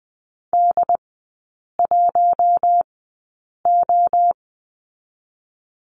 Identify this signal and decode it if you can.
Morse code, heard as a keyed tone, about 20 wpm, spaced as for 8 wpm, 700 Hz. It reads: D1O
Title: 067 Lesson 12e - Letter d - Three Characters 20wpm